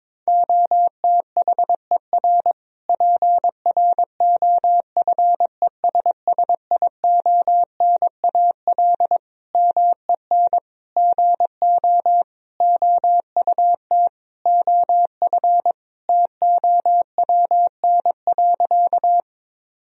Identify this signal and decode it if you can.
Morse code, heard as a keyed tone, about 22 wpm, 700 Hz. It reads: OTHER PROFESSIONAL MEN GO OUT OF TOWN.
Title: Sentences from Top 1000 Words - Review Part 4 - 22wpm